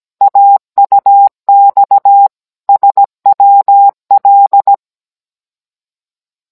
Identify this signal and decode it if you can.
Morse code, heard as a keyed tone, about 17 wpm, 800 Hz. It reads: AUX SWL
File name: Cours-NR19-16mm